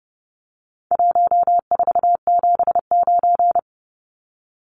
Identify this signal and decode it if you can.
Morse code, heard as a keyed tone, about 30 wpm, 700 Hz. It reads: 1479
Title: Numbers - Contest Style - 4-Digit 30wpm